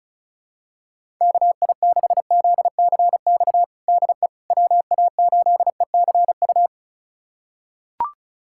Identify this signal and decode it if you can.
Morse code, heard as a keyed tone, about 36 wpm, 700 Hz. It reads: KI6ZCX DE WA8ECU
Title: USA Callsign DE Callsign - 6 Characters - 35wpm